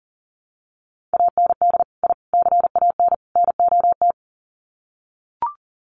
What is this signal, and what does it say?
Morse code, heard as a keyed tone, about 40 wpm, 700 Hz. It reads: AND I CAN NOT
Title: Sets of 4 Words 40wpm